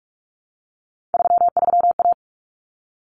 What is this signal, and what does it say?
Morse code, heard as a keyed tone, about 44 wpm, 700 Hz. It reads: 33A
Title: Common Cut Numbers - 3-Digit 45wpm